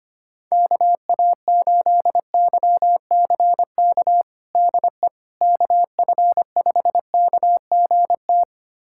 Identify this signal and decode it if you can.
Morse code, heard as a keyed tone, about 25 wpm, 700 Hz. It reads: KA8YCK DE KF5KGT
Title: USA Callsign DE Callsign - 6 Characters - 25wpm